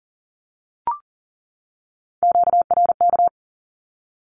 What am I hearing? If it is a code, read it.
Morse code, heard as a keyed tone, about 40 wpm, 700 Hz. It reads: QRK
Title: Q-Codes 40wpm